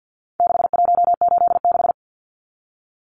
Morse code, heard as a keyed tone, about 50 wpm, 700 Hz. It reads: 6186
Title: Numbers - 4-Digit 50wpm